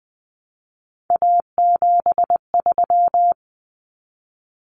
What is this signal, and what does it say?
Morse code, heard as a keyed tone, about 20 wpm, 700 Hz. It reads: A73
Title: Common Cut Numbers - 3-Digit 20wpm